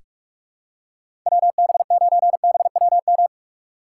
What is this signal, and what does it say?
Morse code, heard as a keyed tone, about 46 wpm, 700 Hz. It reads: WB9BWM
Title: USA Call Signs - 4x Character Spacing - 45wpm